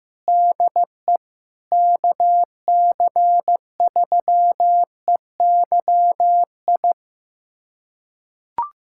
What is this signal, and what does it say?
Morse code, heard as a keyed tone, about 15 wpm, 700 Hz. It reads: DE KC3EYI
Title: USA Callsign DE Callsign - 6 Characters - 15wpm